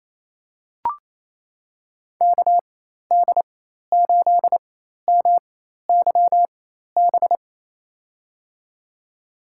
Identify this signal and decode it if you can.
Morse code, heard as a keyed tone, about 28 wpm, spaced as for 13 wpm, 700 Hz. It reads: KD8MYB
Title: USA Call Signs - 6 Characters - 4x Character Spacing - 28wpm